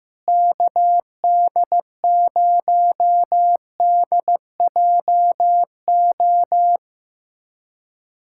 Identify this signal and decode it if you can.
Morse code, heard as a keyed tone, about 15 wpm, 700 Hz. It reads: KD0DJO
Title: USA Callsign DE Callsign 15wpm